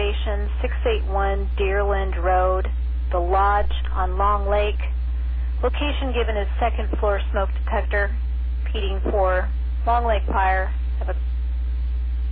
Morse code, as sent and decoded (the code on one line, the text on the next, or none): none